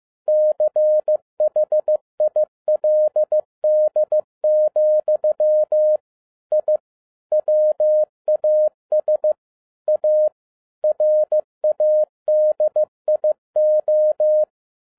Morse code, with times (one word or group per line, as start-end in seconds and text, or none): 0.3-6.0 CHILD,
6.5-6.8 I
7.3-9.3 WAS
9.9-10.3 A
10.8-14.4 RADIO